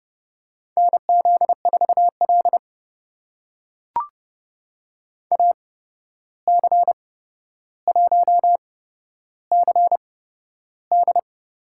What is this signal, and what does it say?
Morse code, heard as a keyed tone, about 30 wpm, spaced as for 12 wpm, 700 Hz. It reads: NZ4L AC1CD
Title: USA Call Signs - 8x Character Spacing - 30wpm